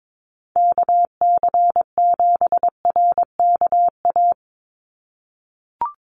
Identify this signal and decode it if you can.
Morse code, heard as a keyed tone, about 22 wpm, 700 Hz. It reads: KC7RKA
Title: USA Call Signs 22wpm